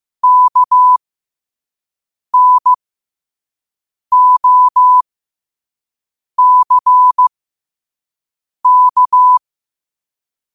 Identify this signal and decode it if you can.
Morse code, heard as a keyed tone, about 15 wpm, spaced as for 5 wpm, 1000 Hz. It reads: KNOCK
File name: qotdfiles / 5wpm